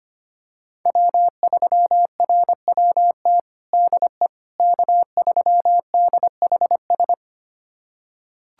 Morse code, heard as a keyed tone, about 25 wpm, 700 Hz. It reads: W3RWT DE K3DHS
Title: USA Callsign DE Callsign - 5 Characters - 25wpm